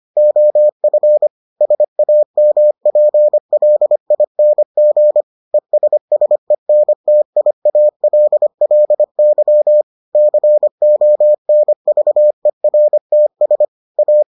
Morse code, heard as a keyed tone, about 25 wpm, 600 Hz. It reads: OF SAMPLING ESSENTIALLY CONVERTS A